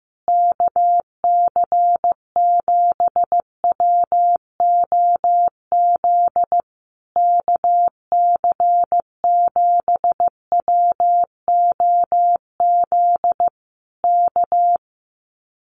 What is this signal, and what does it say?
Morse code, heard as a keyed tone, about 15 wpm, 700 Hz. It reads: KC7WOZ KC7WOZ K
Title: Common QSO Phrases Encore 15wpm